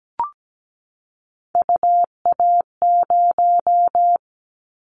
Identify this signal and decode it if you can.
Morse code, heard as a keyed tone, about 17 wpm, 700 Hz. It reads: UA0
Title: Callsign Prefixes for Top 50 DXCC - 17wpm